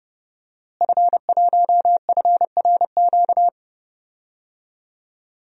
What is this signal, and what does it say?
Morse code, heard as a keyed tone, about 32 wpm, 700 Hz. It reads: F1FRQ
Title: DXCC-100 Call Signs - 5 Characters - Repeated 3x - 30wpm